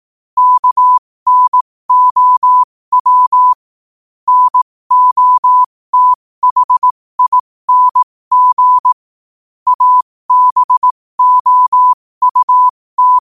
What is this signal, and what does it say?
Morse code, heard as a keyed tone, about 18 wpm, spaced as for 15 wpm, 1000 Hz. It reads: KNOW NOTHING ABOUT
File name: qotdfiles / 15wpm